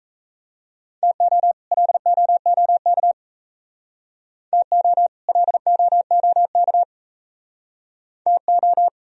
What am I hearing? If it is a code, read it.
Morse code, heard as a keyed tone, about 34 wpm, 700 Hz. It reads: TO LOOK TO LOOK TO
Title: Speed-Racing - Sets of 2 Words 28wpm